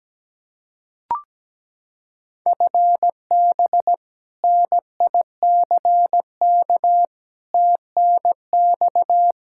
Morse code, heard as a keyed tone, about 17 wpm, 700 Hz. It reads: FB NICK TNX
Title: Practice QSO - Exchange 3 - More Info - 17wpm